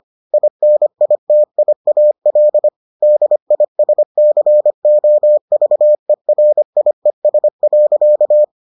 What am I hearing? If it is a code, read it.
Morse code, heard as a keyed tone, about 25 wpm, 600 Hz. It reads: INITIAL DISCOVERIES.